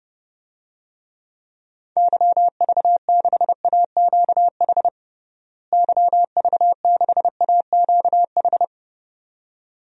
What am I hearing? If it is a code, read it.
Morse code, heard as a keyed tone, about 30 wpm, 700 Hz. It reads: YV6AQH YV6AQH
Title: DXCC-100 Call Signs - 6 Characters - Repeated 2x - 30wpm